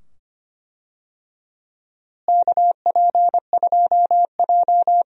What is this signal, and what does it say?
Morse code, heard as a keyed tone, about 25 wpm, 700 Hz. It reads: KP2J
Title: DXCC-100 Call Signs - 4 Characters - Repeated 2x - 25wpm